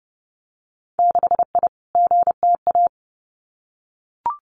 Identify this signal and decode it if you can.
Morse code, heard as a keyed tone, about 30 wpm, 700 Hz. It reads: 6I GTA
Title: Winter Field Day Exchanges Encore 30wpm